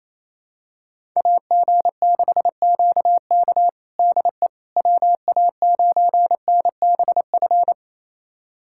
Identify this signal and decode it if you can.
Morse code, heard as a keyed tone, about 28 wpm, 700 Hz. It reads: AG6QK DE WA9NBF
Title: USA Callsign DE Callsign 28wpm